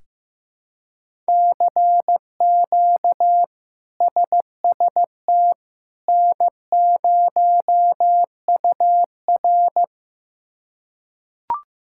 Morse code, heard as a keyed tone, about 15 wpm, 700 Hz. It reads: CQ SST N0UR
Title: K1USN SST QSO Practice - 15wpm